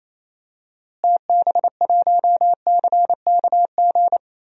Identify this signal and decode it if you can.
Morse code, heard as a keyed tone, about 28 wpm, 700 Hz. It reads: TB1CKG
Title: DXCC-100 Call Signs - 6 Characters - Repeated 3x - 28wpm